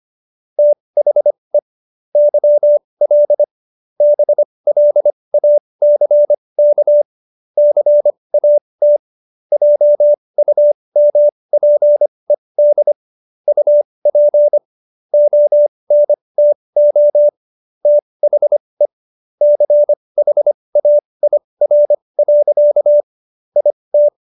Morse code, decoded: THE YL BLACK CAT JUMPED UP ONTO THE CHAIR. IT

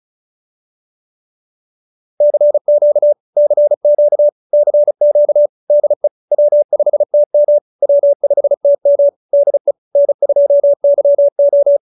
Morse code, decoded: CQ CQ CQ DE W5TM W5TM DE N2YO

